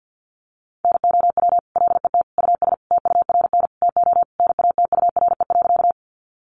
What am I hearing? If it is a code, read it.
Morse code, heard as a keyed tone, about 50 wpm, 700 Hz. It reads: NOW LET US TURN TO NATURE.
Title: Sentences from Top 600 Words 50wpm